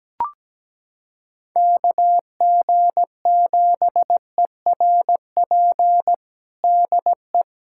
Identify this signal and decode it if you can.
Morse code, heard as a keyed tone, about 17 wpm, 700 Hz. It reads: KG7ERP DE
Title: USA Callsign DE Callsign 17wpm